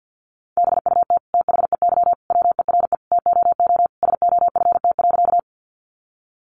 Morse code, heard as a keyed tone, about 50 wpm, 700 Hz. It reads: BUT THEY WERE TOO SOFT.
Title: Sentences from Top 1500 Words 50wpm